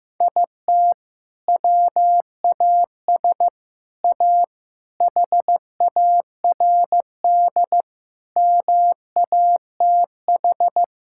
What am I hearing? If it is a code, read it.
Morse code, heard as a keyed tone, about 15 wpm, 700 Hz. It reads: IT WAS A HARD MATH
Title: Sentences from Top 1000 Words - Review Part 6 - 15wpm